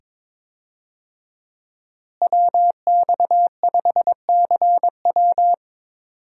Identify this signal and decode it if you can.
Morse code, heard as a keyed tone, about 22 wpm, 700 Hz. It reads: WX5CW
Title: USA Call Signs - 5 Characters - Repeated 2x - 22wpm